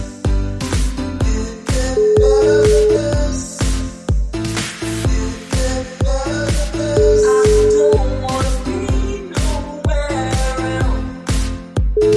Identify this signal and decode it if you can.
Ringback tone (Europe, 425 Hz), with noise.